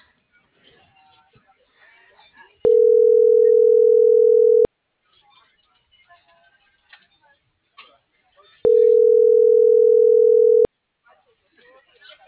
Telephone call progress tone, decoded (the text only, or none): ringback tone